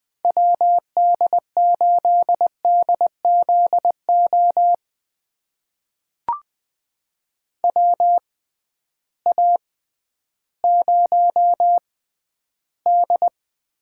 Morse code, heard as a keyed tone, about 20 wpm, spaced as for 12 wpm, 700 Hz. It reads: WD8DZO WA0D
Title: USA Call Signs - 6 Characters - 6x Character Spacing - 20wpm